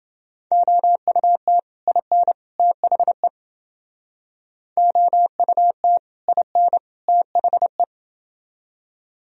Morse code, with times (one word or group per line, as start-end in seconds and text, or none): 0.5-1.6 OUT
1.9-2.3 IN
2.6-3.3 THE
4.8-6.0 OUT
6.3-6.8 IN
7.1-7.8 THE